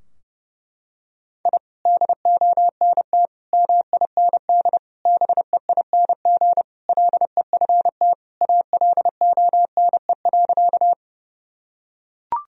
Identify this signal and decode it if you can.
Morse code, heard as a keyed tone, about 30 wpm, 700 Hz. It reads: I DONT MIND BEING LEFT ALONE.
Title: Sentences from Top 1000 Words - Review Part 3 - 30wpm